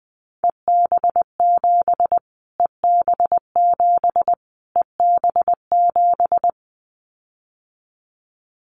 Morse code, heard as a keyed tone, about 20 wpm, 700 Hz. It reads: EB7 EB7 EB7
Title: Callsign Prefixes for Top 25 DXCC - Repeated 3x - 20wpm